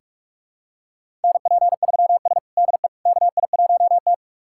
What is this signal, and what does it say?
Morse code, heard as a keyed tone, about 44 wpm, 690 Hz. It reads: NP3S DE KI1T
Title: USA Callsign DE Callsign - 4 Characters - 45wpm